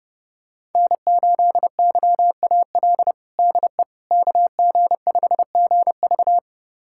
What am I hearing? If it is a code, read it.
Morse code, heard as a keyed tone, about 30 wpm, 700 Hz. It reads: N8YAL DE KG5GV
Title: USA Callsign DE Callsign - 5 Characters - 30wpm